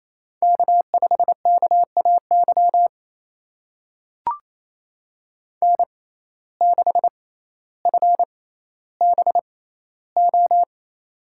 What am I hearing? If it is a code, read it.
Morse code, heard as a keyed tone, about 28 wpm, spaced as for 14 wpm, 700 Hz. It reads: K5KAY N6FBO
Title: USA Call Signs - 5 Characters - 6x Character Spacing - 28wpm